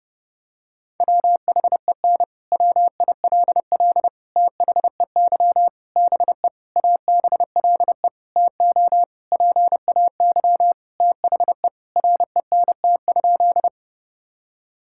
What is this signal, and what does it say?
Morse code, heard as a keyed tone, about 30 wpm, 700 Hz. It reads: WHEN WILL THEY BE ABLE TO PAY THE RENT?
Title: Sentences from Top 1000 Words - Review Part 5 - 30wpm